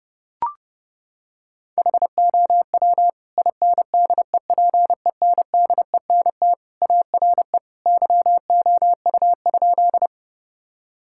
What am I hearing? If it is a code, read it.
Morse code, heard as a keyed tone, about 30 wpm, 700 Hz. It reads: HOW INDEPENDENT ARE YOU?
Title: Sentences from Top 1000 Words - Review Part 6 - 30wpm